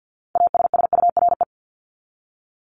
Morse code, heard as a keyed tone, about 50 wpm, 700 Hz. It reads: ASSURE
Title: Top 1501-1750 Words 50wpm